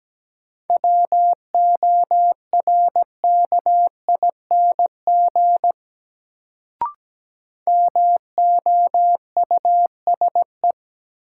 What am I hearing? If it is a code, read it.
Morse code, heard as a keyed tone, about 17 wpm, 700 Hz. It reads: WORKING MOUSE
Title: Sentences from Top 700 Words - New Word Drill - 17wpm